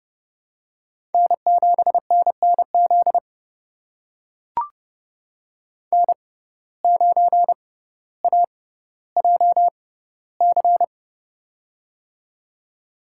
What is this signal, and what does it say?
Morse code, heard as a keyed tone, about 32 wpm, spaced as for 16 wpm, 700 Hz. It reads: N7NNZ N9AJC
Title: USA Call Signs - 5 Characters - 6x Character Spacing - 30wpm